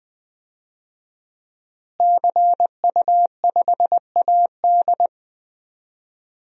Morse code, heard as a keyed tone, about 20 wpm, 700 Hz. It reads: CU5AD
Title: DXCC-100 Call Signs - 5 Characters - Repeated 2x - 20wpm